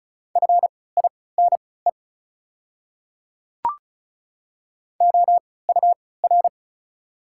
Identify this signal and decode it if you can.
Morse code, heard as a keyed tone, about 34 wpm, spaced as for 20 wpm, 690 Hz. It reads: FINE OUR
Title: 3x Character Spacing - Top 500 Words - 35wpm